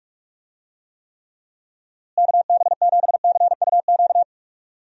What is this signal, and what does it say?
Morse code, heard as a keyed tone, about 43 wpm, 700 Hz. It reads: KB7CUQ
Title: USA Call Signs - 6 Characters - Repeated 3x - 45wpm